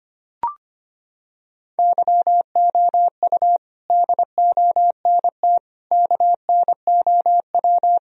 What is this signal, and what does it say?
Morse code, heard as a keyed tone, about 25 wpm, 700 Hz. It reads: YOU DONT KNOW
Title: Sentences from Top 1000 Words - Review Part 1 - 25wpm